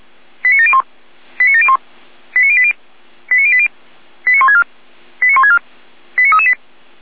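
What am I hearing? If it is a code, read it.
Five-tone selective call (ZVEI), heard as quick runs of five tones, tones of about 80 ms. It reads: 89701 89701 89080 89080 89164 89164 89338
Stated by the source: Five-tone ZVEI selective call received on SDR.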